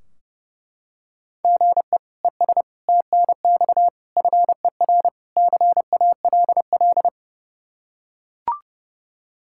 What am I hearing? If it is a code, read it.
Morse code, heard as a keyed tone, about 30 wpm, 700 Hz. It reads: GE ES TNX FER CALL
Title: K1USN SST QSO Practice 30wpm